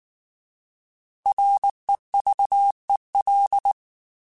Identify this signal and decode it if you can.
Morse code, heard as a keyed tone, about 19 wpm, 800 Hz. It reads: REVEL
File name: Alphabetical / Alphabetical mp3s / Rs @18